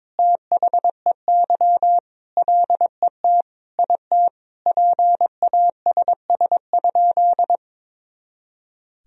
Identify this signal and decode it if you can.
Morse code, heard as a keyed tone, about 22 wpm, 700 Hz. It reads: THEY LET IT PASS?
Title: Sentences from Top 400 Words and Abbreviations 22wpm